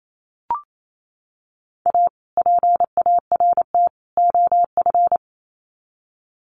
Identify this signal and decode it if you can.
Morse code, heard as a keyed tone, about 28 wpm, 700 Hz. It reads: A PART OF
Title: Sets of 3 Words 28wpm